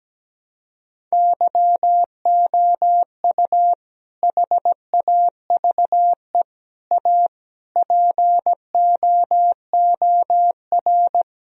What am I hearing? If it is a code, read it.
Morse code, heard as a keyed tone, about 17 wpm, 700 Hz. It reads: YOU HAVE A POOR
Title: Sentences from Top 1000 Words - Review Part 3 - 17wpm